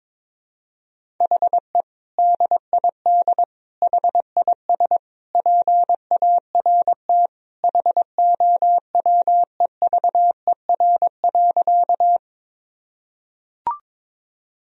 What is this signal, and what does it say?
Morse code, heard as a keyed tone, about 22 wpm, 700 Hz. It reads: HE DID HIS PART HOWEVER.
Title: Sentences from Top 1000 Words - Review Part 1 - 22wpm